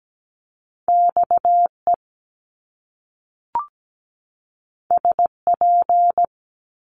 Morse code, heard as a keyed tone, about 17 wpm, 700 Hz. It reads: XE SP
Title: Top Two Letter Combinations in Words 17wpm